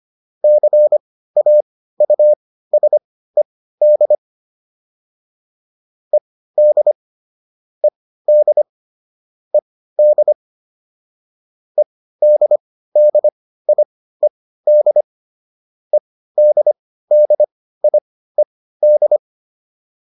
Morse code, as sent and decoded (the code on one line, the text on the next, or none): -.-. .- ..- ... -.. / . -.. / . -.. / . -.. / . -.. -.. .. -.. / . -.. -.. .. -..
CAUSED ED ED ED EDDIED EDDIED